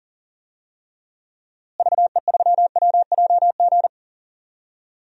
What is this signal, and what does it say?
Morse code, heard as a keyed tone, about 40 wpm, 700 Hz. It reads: VE3WJG
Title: DXCC-100 Call Signs - 6 Characters - Repeated 2x - 40wpm